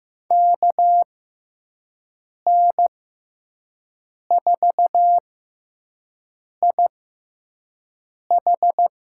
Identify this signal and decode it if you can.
Morse code, heard as a keyed tone, about 15 wpm, spaced as for 5 wpm, 700 Hz. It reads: KN4IH
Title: USA Call Signs - 6x Character Spacing - 15wpm